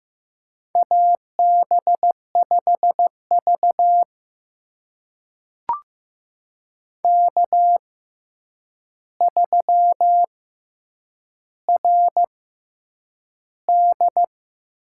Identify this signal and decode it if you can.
Morse code, heard as a keyed tone, about 15 wpm, spaced as for 8 wpm, 700 Hz. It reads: AB5V K3RD
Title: USA Call Signs - 4 Characters - 6x Character Spacing - 15wpm